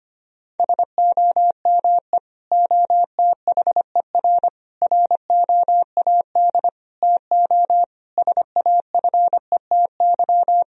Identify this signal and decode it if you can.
Morse code, heard as a keyed tone, about 25 wpm, 700 Hz. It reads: SOME OTHER ROAD TO SAFETY
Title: Sentences from Top 1000 Words - Review Part 4 - 25wpm